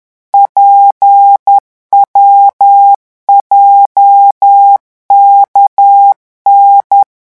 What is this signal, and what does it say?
Morse code, heard as a keyed tone, about 10 wpm, 800 Hz. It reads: PWJKN